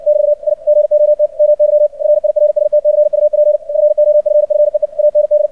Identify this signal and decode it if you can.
Morse code, heard as a keyed tone, about 56 wpm, 600 Hz. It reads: <BT> I GOT MY RIRST CUP FOFFEE AND
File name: archived / CRSnet